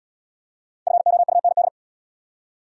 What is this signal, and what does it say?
Morse code, heard as a keyed tone, about 72 wpm, 700 Hz. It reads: FOUND